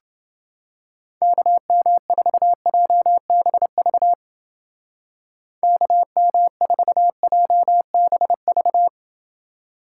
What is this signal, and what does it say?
Morse code, heard as a keyed tone, about 28 wpm, 700 Hz. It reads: KM4JBV KM4JBV